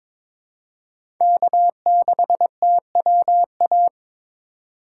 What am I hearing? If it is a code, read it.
Morse code, heard as a keyed tone, about 22 wpm, 700 Hz. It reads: K6TWA
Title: USA Call Signs - 5 Characters - 8x Character Spacing - 22wpm